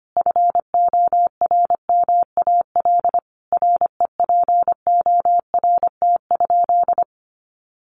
Morse code, heard as a keyed tone, about 25 wpm, 700 Hz. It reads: FORMAL REPORT?
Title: Sentences from Top 1300 Words 25wpm